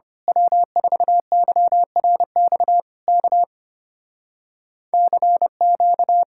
Morse code, heard as a keyed tone, about 25 wpm, 700 Hz. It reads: W4YRX K CQ
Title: Speed-Racing Common QSO Phrases - 25wpm